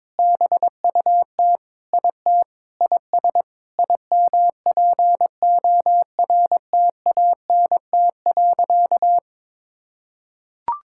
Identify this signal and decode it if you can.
Morse code, heard as a keyed tone, about 22 wpm, 700 Hz. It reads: BUT IT IS IMPORTANT.
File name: practice / 22wpm / Sentences with 8-Word Phrases - 22wpm